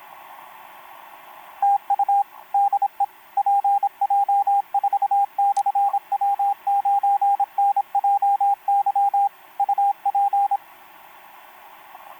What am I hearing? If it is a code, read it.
Morse code, heard as a keyed tone, about 26 wpm, 800 Hz, with radio noise. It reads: TU DE PJ4/W9NJY UP
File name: pj4-w9njy f5upo etc 12m cw 28mar2011 1924